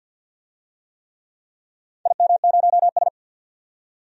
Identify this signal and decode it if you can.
Morse code, heard as a keyed tone, about 47 wpm, 700 Hz. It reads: IM0S
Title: DXCC-100 Call Signs - 4 Characters - 50wpm